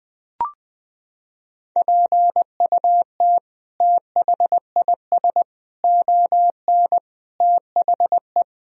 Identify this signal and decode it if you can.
Morse code, heard as a keyed tone, about 20 wpm, 700 Hz. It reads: PUT THIS ON THE
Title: Sentences from Top 1000 Words - Review Part 1 - 20wpm